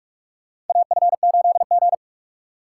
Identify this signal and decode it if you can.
Morse code, heard as a keyed tone, about 45 wpm, 700 Hz. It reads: AF8G
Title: USA Call Signs - 3x Character Spacing - 45wpm